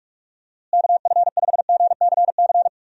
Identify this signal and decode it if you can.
Morse code, heard as a keyed tone, about 46 wpm, 700 Hz. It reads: KF5GCC
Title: USA Call Signs - 6 Characters - Repeated 2x - 45wpm